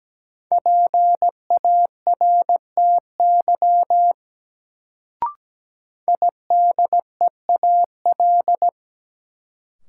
Morse code, heard as a keyed tone, about 17 wpm, 700 Hz. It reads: PARTY IDEAL
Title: Sentences from Top 1400 Words - New Word Drill Alt - 17wpm